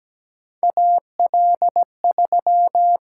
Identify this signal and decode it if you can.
Morse code, heard as a keyed tone, about 17 wpm, 700 Hz. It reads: AL3
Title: USA Call Signs - 4 Characters - 6x Character Spacing - 17wpm